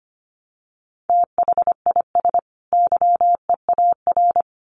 Morse code, heard as a keyed tone, about 25 wpm, 700 Hz. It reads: THIS YEAR
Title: Sets of 2 Words 25wpm